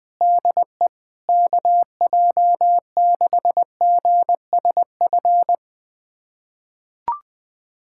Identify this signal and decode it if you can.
Morse code, heard as a keyed tone, about 20 wpm, 700 Hz. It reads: DE KJ6GSF
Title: USA Callsign DE Callsign - 6 Characters - 20wpm